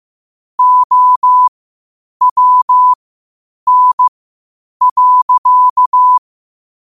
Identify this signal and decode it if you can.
Morse code, heard as a keyed tone, about 15 wpm, spaced as for 8 wpm, 1000 Hz. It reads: OWN.
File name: qotdfiles / 8wpm